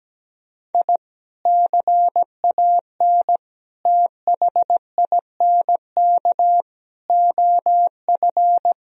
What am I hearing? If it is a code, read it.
Morse code, heard as a keyed tone, about 17 wpm, 700 Hz. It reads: I CAN THINK OF
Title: Sentences from Top 1000 Words - Review Part 1 - 17wpm